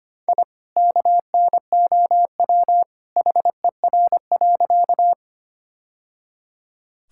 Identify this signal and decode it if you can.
Morse code, heard as a keyed tone, about 25 wpm, 700 Hz. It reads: I KNOW HER.
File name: practice / 25wpm / Sentences with 2-Word Phrases - 25wpm